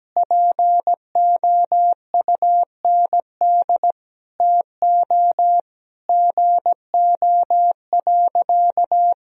Morse code, heard as a keyed tone, about 17 wpm, 700 Hz. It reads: POUND TO GO.
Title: Sentences from Top 1000 Words - Review Part 2 - 17wpm